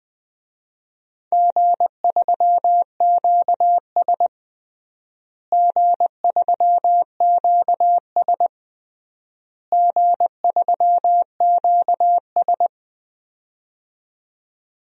Morse code, heard as a keyed tone, about 20 wpm, 700 Hz. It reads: G3QS G3QS G3QS
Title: DXCC-100 Call Signs - 4 Characters - Repeated 3x - 20wpm